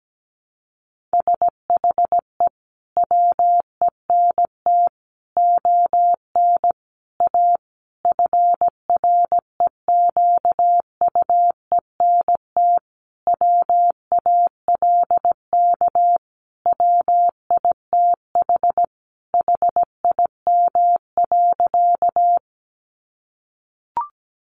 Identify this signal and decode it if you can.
Morse code, heard as a keyed tone, about 17 wpm, 700 Hz. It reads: SHE WENT ON A FREQUENT WALK WITH HIM.
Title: Sentences from Top 1400 Words 17wpm